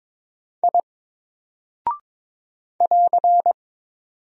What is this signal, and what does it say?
Morse code, heard as a keyed tone, about 22 wpm, 700 Hz. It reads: I <AR>
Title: Single Character (Letter:Number:Punctuation:Prosign) 22wpm